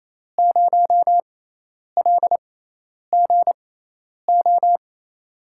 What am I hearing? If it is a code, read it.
Morse code, heard as a keyed tone, about 28 wpm, spaced as for 10 wpm, 700 Hz. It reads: 0LGO
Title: USA Call Signs - 6 Characters - 6x Character Spacing - 28wpm